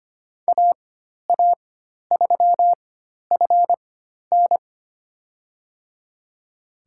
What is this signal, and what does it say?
Morse code, heard as a keyed tone, about 25 wpm, spaced as for 12 wpm, 700 Hz. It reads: AA3FN
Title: USA Call Signs - 4x Character Spacing - 25wpm